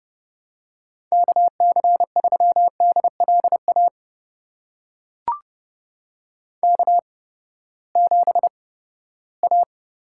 Morse code, heard as a keyed tone, about 30 wpm, spaced as for 17 wpm, 700 Hz. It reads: KC3DLA K7A
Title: USA Call Signs - 8x Character Spacing - 30wpm